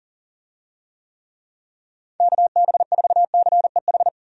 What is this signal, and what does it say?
Morse code, heard as a keyed tone, about 40 wpm, 700 Hz. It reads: KB4CEH
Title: USA Call Signs - 6 Characters - 40wpm